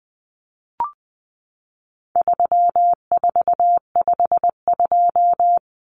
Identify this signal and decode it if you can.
Morse code, heard as a keyed tone, about 20 wpm, 700 Hz. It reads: 3452
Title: Numbers - 4-Digit 20wpm